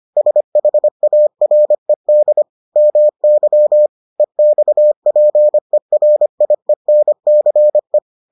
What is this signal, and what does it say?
Morse code, heard as a keyed tone, about 25 wpm, 600 Hz. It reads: SHARED MY EXPERIENCE